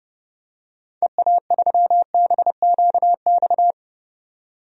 Morse code, heard as a keyed tone, about 30 wpm, 700 Hz. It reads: EA3BQX
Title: DXCC-100 Call Signs - 6 Characters - 30wpm